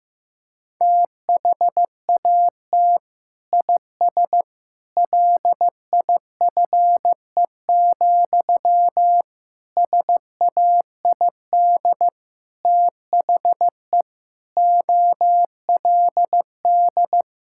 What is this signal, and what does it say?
Morse code, heard as a keyed tone, about 15 wpm, 700 Hz. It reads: THAT IS LIFE, SAID THE OLD
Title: Sentences from Top 1000 Words - Review Part 6 - 15wpm